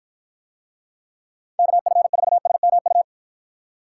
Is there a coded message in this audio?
Yes: Morse code from 1.6 to 3.0 s, heard as a keyed tone, about 50 wpm, 700 Hz. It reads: KF4SMU